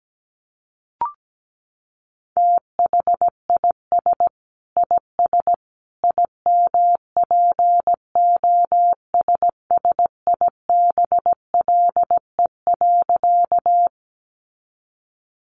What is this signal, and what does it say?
Morse code, heard as a keyed tone, about 17 wpm, 700 Hz. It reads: THIS IS IMPOSSIBLE.
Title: A Sound Of Thunder 17wpm